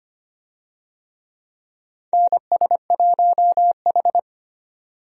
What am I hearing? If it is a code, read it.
Morse code, heard as a keyed tone, about 25 wpm, 700 Hz. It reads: NS1H
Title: USA Call Signs - 4 Characters - 25wpm